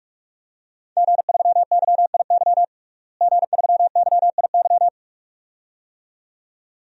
Morse code, heard as a keyed tone, about 46 wpm, 700 Hz. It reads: G3YIY G3YIY
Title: DXCC-100 Call Signs - 5 Characters - Repeated 2x - 45wpm